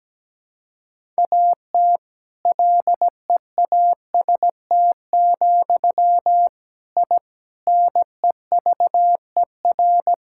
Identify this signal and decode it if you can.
Morse code, heard as a keyed tone, about 17 wpm, 700 Hz. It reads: AT LEAST, I NEVER
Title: Sentences from Top 1000 Words - Review Part 5 - 17wpm